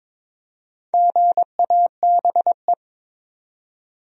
Morse code, heard as a keyed tone, about 22 wpm, 700 Hz. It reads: GABE
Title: USA Common Names - Max 4 Letters - 22wpm